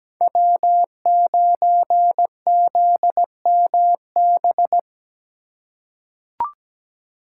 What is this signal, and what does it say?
Morse code, heard as a keyed tone, about 17 wpm, 700 Hz. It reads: W9ZMB
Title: USA Call Signs - 5 Characters - 4x Character Spacing - 17wpm